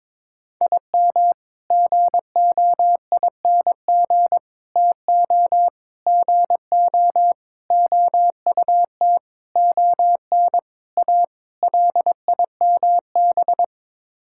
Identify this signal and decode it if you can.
Morse code, heard as a keyed tone, about 22 wpm, 700 Hz. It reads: IM GOING TO GO OUT ON A LIMB
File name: practice / 22wpm / Sets of 8 Words - 22wpm